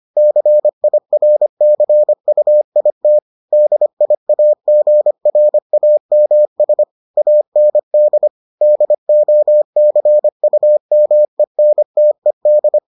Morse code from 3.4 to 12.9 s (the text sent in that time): DIAGRAMS AND DOCUMENTED